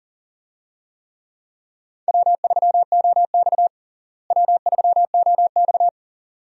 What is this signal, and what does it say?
Morse code, heard as a keyed tone, about 40 wpm, 700 Hz. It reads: W3OX W3OX
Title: USA Call Signs - 4 Characters - Repeated 2x - 40wpm